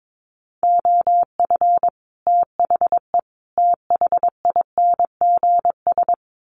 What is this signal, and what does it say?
Morse code, heard as a keyed tone, about 22 wpm, 700 Hz. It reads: OF THE THINGS
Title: Sets of 5 Words Encore 22wpm